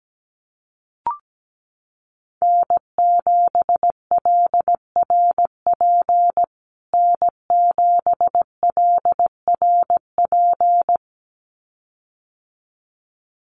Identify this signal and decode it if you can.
Morse code, heard as a keyed tone, about 17 wpm, 700 Hz. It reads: N7LRP N7LRP
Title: USA Call Signs Repeated 2x 17wpm